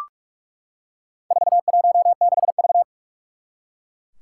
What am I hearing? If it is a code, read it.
Morse code, heard as a keyed tone, about 44 wpm, 700 Hz. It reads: 416V
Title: Four Challenging Letter-Number Combinations 45wpm